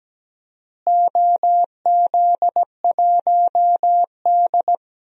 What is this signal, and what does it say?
Morse code, heard as a keyed tone, about 17 wpm, 700 Hz. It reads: OZ1D
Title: DXCC-100 Call Signs - 4 Characters - Repeated 2x - 17wpm